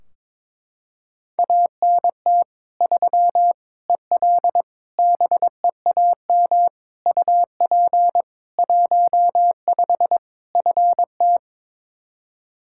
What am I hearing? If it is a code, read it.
Morse code, heard as a keyed tone, about 22 wpm, 700 Hz. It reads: ANT 3 EL BEAM UP 15 FT
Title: Practice QSO - Exchange 2 - Info - 22wpm